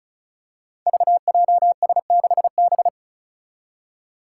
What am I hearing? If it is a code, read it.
Morse code, heard as a keyed tone, about 36 wpm, 700 Hz. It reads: VJS6B
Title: Five Challenging Letter-Number Combinations 35wpm